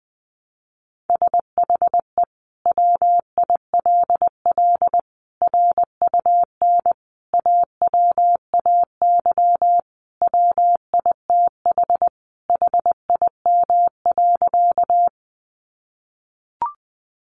Morse code, heard as a keyed tone, about 20 wpm, 700 Hz. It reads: SHE WILL RUN AWAY WITH HIM.
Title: Sentences from Top 600 Words 20wpm